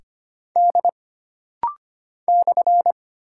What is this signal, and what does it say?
Morse code, heard as a keyed tone, about 25 wpm, 700 Hz. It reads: D /
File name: Single Character (Letter:Number:Punctuation:Prosign) - Rapid-Fire - 25wpm